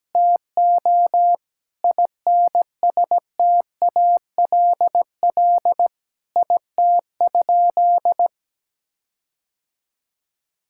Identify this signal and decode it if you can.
Morse code, heard as a keyed tone, about 17 wpm, 700 Hz. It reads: TO INSTALL IT?
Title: Sentences from Top 1600 Words 17wpm